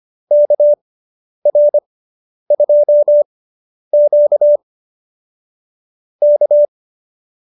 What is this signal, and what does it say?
Morse code, heard as a keyed tone, about 25 wpm, spaced as for 10 wpm, 590 Hz. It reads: KR2Q K
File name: QSO220 10